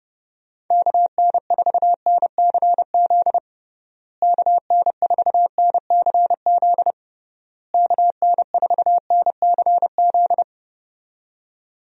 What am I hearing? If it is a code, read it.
Morse code, heard as a keyed tone, about 30 wpm, 700 Hz. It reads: KN4NCZ KN4NCZ KN4NCZ
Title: USA Call Signs - 6 Characters - Repeated 3x - 30wpm